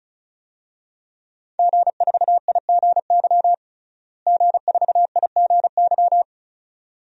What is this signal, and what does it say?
Morse code, heard as a keyed tone, about 34 wpm, 700 Hz. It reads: G4IGY G4IGY
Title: DXCC-100 Call Signs - 5 Characters - Repeated 2x - 35wpm